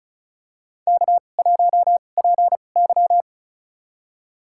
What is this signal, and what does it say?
Morse code, heard as a keyed tone, about 36 wpm, spaced as for 25 wpm, 700 Hz. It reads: K1PY
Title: USA Call Signs - 2x Character Spacing - 35wpm